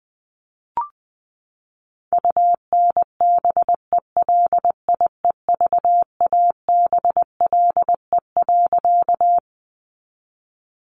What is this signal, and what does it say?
Morse code, heard as a keyed tone, about 19 wpm, 700 Hz. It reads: UNBELIEVABLE.
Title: A Sound Of Thunder 20wpm